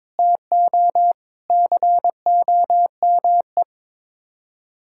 Morse code, heard as a keyed tone, about 22 wpm, 700 Hz. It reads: TO COME
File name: practice / 22wpm / Sets of 6 Words - 22wpm